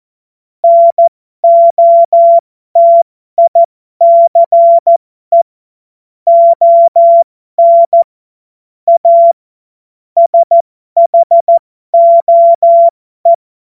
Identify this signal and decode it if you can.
Morse code, heard as a keyed tone, about 14 wpm, spaced as for 12 wpm, 680 Hz. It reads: NOTICE ON A SHOE
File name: Morsekurs / Witze